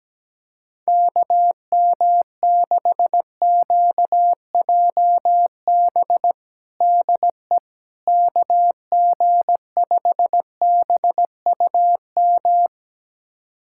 Morse code, heard as a keyed tone, about 17 wpm, 700 Hz. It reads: KM6QJB DE KG5BUM
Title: USA Callsign DE Callsign - 6 Characters - 17wpm